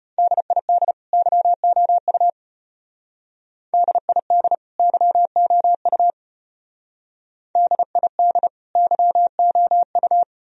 Morse code, heard as a keyed tone, about 33 wpm, 700 Hz. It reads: DID YOU DID YOU DID YOU